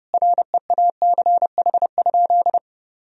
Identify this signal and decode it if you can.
Morse code, heard as a keyed tone, about 30 wpm, 700 Hz. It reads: REACH?
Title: Sentences from Top 1000 Words - Review Part 4 - 30wpm